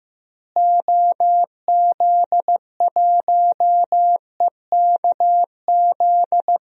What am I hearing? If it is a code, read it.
Morse code, heard as a keyed tone, about 15 wpm, 700 Hz. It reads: OZ1EKZ
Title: DXCC-100 Call Signs - 6 Characters - Repeated 3x - 15wpm